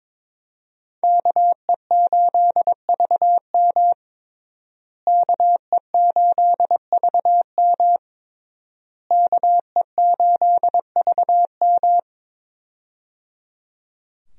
Morse code, heard as a keyed tone, about 22 wpm, 700 Hz. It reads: KE8VM KE8VM KE8VM
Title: USA Call Signs - 5 Characters - Repeated 3x - 22wpm